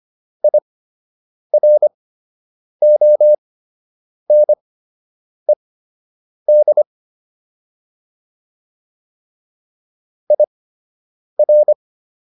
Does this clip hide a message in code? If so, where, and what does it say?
Yes: Morse code from 0.4 to 11.7 s, heard as a keyed tone, about 25 wpm, spaced as for 8 wpm, 600 Hz. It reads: IRONED IR